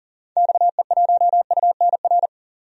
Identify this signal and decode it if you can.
Morse code, heard as a keyed tone, about 40 wpm, 700 Hz. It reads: XE1UNR
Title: DXCC-100 Call Signs - 6 Characters - 40wpm